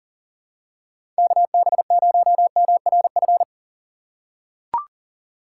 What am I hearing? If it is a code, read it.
Morse code, heard as a keyed tone, about 38 wpm, 700 Hz. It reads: KB0MRF
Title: USA Call Signs - 6 Characters - 2x Character Spacing - 40wpm